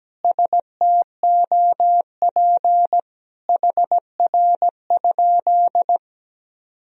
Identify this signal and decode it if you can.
Morse code, heard as a keyed tone, about 17 wpm, 700 Hz. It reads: STOP HR?
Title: Sentences from Top 300 Words and Abbreviations 17wpm